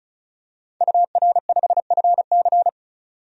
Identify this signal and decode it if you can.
Morse code, heard as a keyed tone, about 36 wpm, 700 Hz. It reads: UR5FC
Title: DXCC-100 Call Signs - 5 Characters - 35wpm